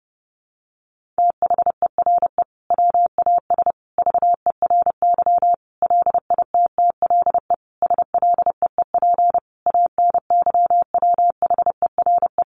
Morse code, heard as a keyed tone, about 30 wpm, 700 Hz. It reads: THERE WAS VERY LITTLE SLEEP ANYWHERE
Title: Sentences from Top 1400 Words 30wpm